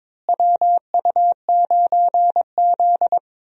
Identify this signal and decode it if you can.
Morse code, heard as a keyed tone, about 22 wpm, 700 Hz. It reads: WU9Z
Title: USA Call Signs - 4 Characters - Repeated 3x - 22wpm